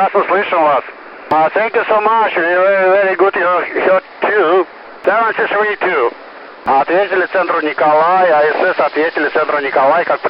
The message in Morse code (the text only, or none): none